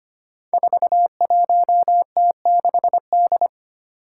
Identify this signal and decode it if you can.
Morse code, heard as a keyed tone, about 25 wpm, 700 Hz. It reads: 41T6D
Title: Five Challenging Letter-Number Combinations 25wpm